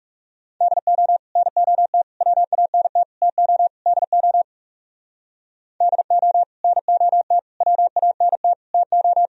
Morse code, heard as a keyed tone, about 38 wpm, 700 Hz. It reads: DO NOT WANT TO DO DO NOT WANT TO